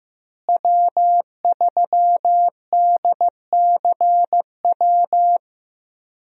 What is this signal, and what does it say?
Morse code, heard as a keyed tone, about 15 wpm, 700 Hz. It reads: W3DCW
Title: USA Call Signs - 5 Characters - Repeated 3x - 15wpm